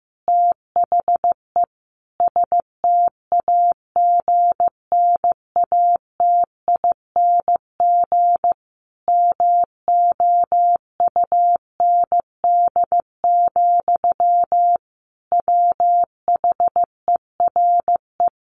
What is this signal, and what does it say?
Morse code, heard as a keyed tone, about 15 wpm, 700 Hz. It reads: THE STAGNATING MOUND, WHERE
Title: A Sound Of Thunder 15wpm